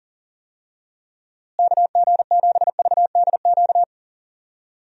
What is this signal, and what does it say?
Morse code, heard as a keyed tone, about 40 wpm, 700 Hz. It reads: KG7VDQ
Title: USA Call Signs - 6 Characters - Repeated 2x - 40wpm